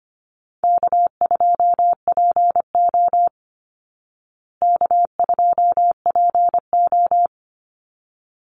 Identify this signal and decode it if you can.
Morse code, heard as a keyed tone, about 25 wpm, 700 Hz. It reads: K2PO K2PO
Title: Easy Call Signs Repeated 2x - 25wpm